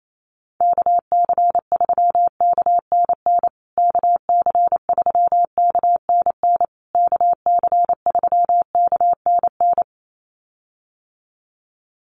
Morse code, heard as a keyed tone, about 28 wpm, 700 Hz. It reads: KC3KNN KC3KNN KC3KNN
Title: USA Call Signs Repeated 3x 28wpm